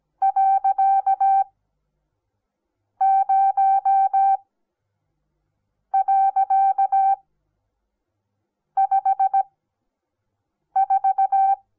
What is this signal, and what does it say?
Morse code, heard as a keyed tone, about 17 wpm, spaced as for 5 wpm, 780 Hz. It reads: .0.54
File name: Onlinelehrgang / morsen / DJ4UF Morsen08